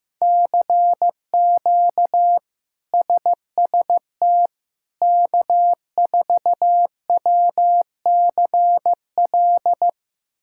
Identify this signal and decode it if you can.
Morse code, heard as a keyed tone, about 15 wpm, 700 Hz. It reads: CQ SST K4WCL
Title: K1USN SST QSO Practice - 15wpm